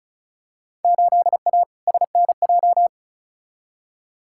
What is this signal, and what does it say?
Morse code, heard as a keyed tone, about 36 wpm, 700 Hz. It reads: 8A SNJ
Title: Field Day Exchanges Encore - 35wpm